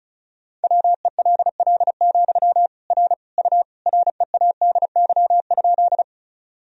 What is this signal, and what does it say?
Morse code, heard as a keyed tone, about 36 wpm, 700 Hz. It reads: WELL, R U READY?
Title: Sentences from Top 400 Words and Abbreviations 35wpm